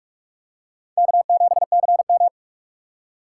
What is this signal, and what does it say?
Morse code, heard as a keyed tone, about 46 wpm, 700 Hz. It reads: K7CM
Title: USA Call Signs - 4 Characters - 3x Character Spacing - 45wpm